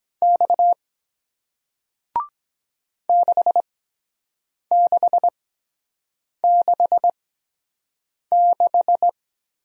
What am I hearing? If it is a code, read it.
Morse code, heard as a keyed tone, about 22 wpm, spaced as for 7 wpm, 700 Hz. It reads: X 6666